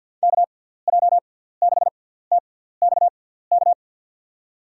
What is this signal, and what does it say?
Morse code, heard as a keyed tone, about 50 wpm, spaced as for 17 wpm, 700 Hz. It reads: KJ6TXK
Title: USA Call Signs - 6 Characters - 6x Character Spacing - 50wpm